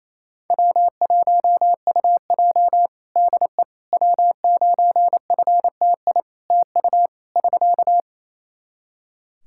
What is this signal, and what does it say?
Morse code, heard as a keyed tone, about 28 wpm, 700 Hz. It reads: W1UJ DE W9FTI TU <SK>
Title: Practice QSO - Exchange 4 - Ending - 28wpm